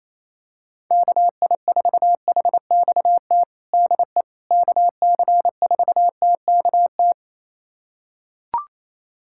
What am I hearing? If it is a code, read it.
Morse code, heard as a keyed tone, about 28 wpm, 700 Hz. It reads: KI4HXT DE KC4TKT
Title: USA Callsign DE Callsign - 6 Characters - 28wpm